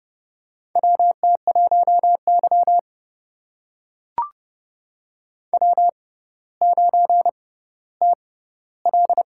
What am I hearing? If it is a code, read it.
Morse code, heard as a keyed tone, about 30 wpm, spaced as for 15 wpm, 700 Hz. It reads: WT1Y W9TL
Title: USA Call Signs - 4 Characters - 6x Character Spacing - 30wpm